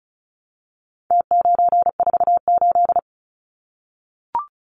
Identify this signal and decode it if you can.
Morse code, heard as a keyed tone, about 34 wpm, 700 Hz. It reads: T948
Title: Numbers - Contest Style - 4-Digit 35wpm